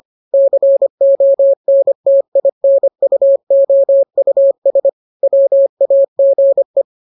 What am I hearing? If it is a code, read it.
Morse code, heard as a keyed tone, about 25 wpm, 550 Hz. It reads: CONTINUOUS WAGE
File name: jokes / joke 2 25 0000